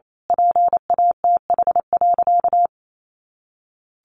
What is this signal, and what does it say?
Morse code, heard as a keyed tone, about 28 wpm, 700 Hz. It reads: PATH.
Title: A Sound Of Thunder 28wpm